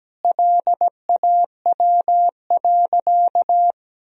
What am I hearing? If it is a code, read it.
Morse code, heard as a keyed tone, about 17 wpm, 700 Hz. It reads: LAW.
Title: Sentences from Top 1000 Words - Review Part 4 - 17wpm